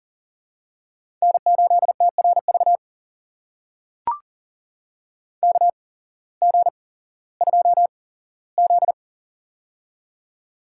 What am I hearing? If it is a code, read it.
Morse code, heard as a keyed tone, about 40 wpm, spaced as for 19 wpm, 700 Hz. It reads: N8TRV KG2Z
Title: USA Call Signs - 8x Character Spacing - 40wpm